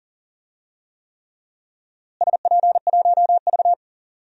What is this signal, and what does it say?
Morse code, heard as a keyed tone, about 37 wpm, 700 Hz. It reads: SP1V
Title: DXCC-100 Call Signs - 4 Characters - Repeated 3x - 40wpm